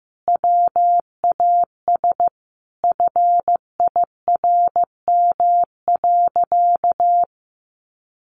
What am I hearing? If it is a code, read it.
Morse code, heard as a keyed tone, about 14 wpm, 700 Hz. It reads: WAS FIRM.
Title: Sentences from Top 1400 Words 15wpm